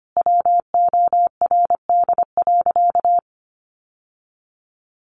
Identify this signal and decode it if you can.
Morse code, heard as a keyed tone, about 25 wpm, 700 Hz. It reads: WORD.
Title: Sentences from Top 1400 Words 25wpm